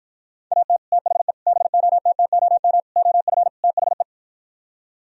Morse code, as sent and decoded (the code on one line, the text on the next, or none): .- - / - .... / -... --- - - --- -- / --- ..-. / - .... .
AT THE BOTTOM OF THE